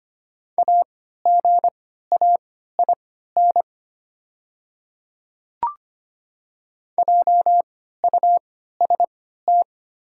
Morse code, heard as a keyed tone, about 25 wpm, spaced as for 14 wpm, 700 Hz. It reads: AGAIN JUST
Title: 3x Character Spacing - Top 200 Words - 25wpm